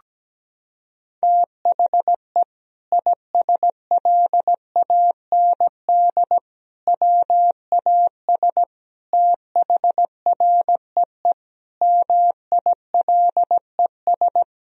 Morse code, heard as a keyed tone, about 17 wpm, 700 Hz. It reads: THE ISLAND WAS THREE MILES
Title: Sentences from Top 1000 Words - Review Part 3 - 17wpm